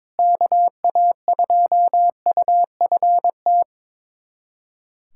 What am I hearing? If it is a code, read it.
Morse code, heard as a keyed tone, about 21 wpm, 700 Hz. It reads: KA2UFT
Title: USA Callsign DE Callsign 22wpm